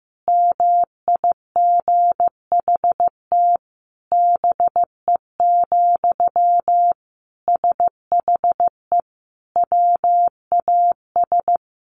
Morse code, heard as a keyed tone, about 15 wpm, 700 Hz. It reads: MIGHT BE, SHE WAS
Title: Sentences from Top 1500 Words 15wpm